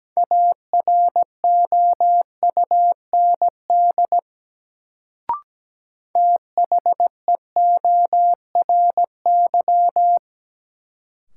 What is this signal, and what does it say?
Morse code, heard as a keyed tone, about 17 wpm, 700 Hz. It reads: AROUND THEORY
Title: Sentences from Top 600 Words - New Word Drill - 17wpm